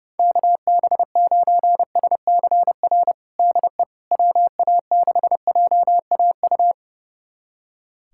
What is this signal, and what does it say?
Morse code, heard as a keyed tone, about 30 wpm, 700 Hz. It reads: KB9SCR DE WA6JAU
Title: USA Callsign DE Callsign - 6 Characters - 30wpm